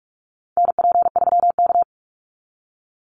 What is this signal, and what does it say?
Morse code, heard as a keyed tone, about 44 wpm, 700 Hz. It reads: NP3K
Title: Call Signs Repeated 3x - 45wpm